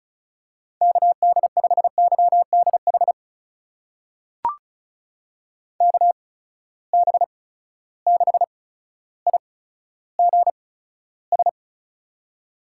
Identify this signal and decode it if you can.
Morse code, heard as a keyed tone, about 36 wpm, spaced as for 15 wpm, 700 Hz. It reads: KD5YDH KB6IGS
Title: USA Call Signs - 6 Characters - 8x Character Spacing - 35wpm